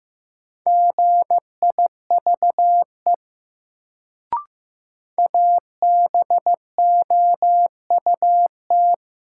Morse code, heard as a keyed tone, about 15 wpm, 700 Hz. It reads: GIVE ABOUT
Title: Sets of 2 Words - New Word Drill - 15wpm